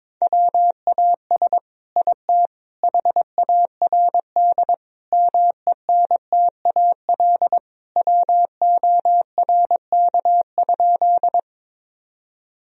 Morse code, heard as a keyed tone, about 22 wpm, 700 Hz. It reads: WAS IT HARD MENTAL WORK?
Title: Sentences from Top 1000 Words - Review Part 4 - 22wpm